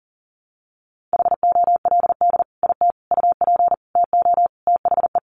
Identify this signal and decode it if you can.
Morse code, heard as a keyed tone, about 40 wpm, 700 Hz. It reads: HOLD IT UP TO THE
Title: Sentences from Top 400 Words 40wpm